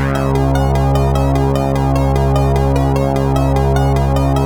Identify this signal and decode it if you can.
There is no code.